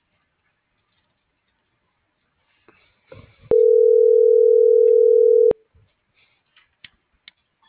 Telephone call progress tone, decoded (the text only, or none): ringback tone